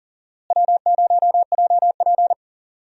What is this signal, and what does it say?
Morse code, heard as a keyed tone, about 40 wpm, 700 Hz. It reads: W0JP
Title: USA Call Signs - 4 Characters - Repeated 2x - 40wpm